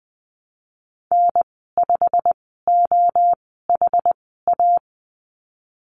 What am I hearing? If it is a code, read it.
Morse code, heard as a keyed tone, about 20 wpm, spaced as for 15 wpm, 700 Hz. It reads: N5OHA
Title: 167 Lesson 27g - Number 7 - Call Signs 20wpm